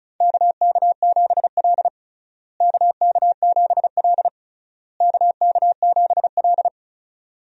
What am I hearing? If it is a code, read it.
Morse code, heard as a keyed tone, about 36 wpm, 700 Hz. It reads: KK7L KK7L KK7L
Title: USA Call Signs - 4 Characters - Repeated 3x - 35wpm